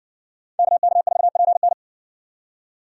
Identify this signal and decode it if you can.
Morse code, heard as a keyed tone, about 60 wpm, 700 Hz. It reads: KK4PN